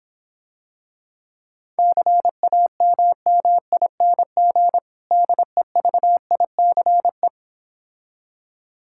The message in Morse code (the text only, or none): CAMMING DEVICE